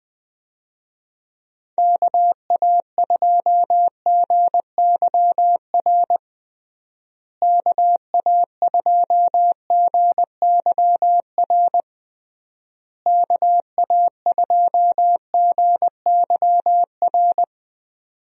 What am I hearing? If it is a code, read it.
Morse code, heard as a keyed tone, about 21 wpm, 700 Hz. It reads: KA2GYR KA2GYR KA2GYR
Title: USA Call Signs - 6 Characters - Repeated 3x - 20wpm